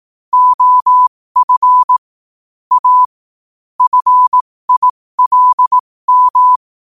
Morse code, heard as a keyed tone, about 18 wpm, spaced as for 15 wpm, 1000 Hz. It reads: OF A FILM
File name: qotdfiles / 15wpm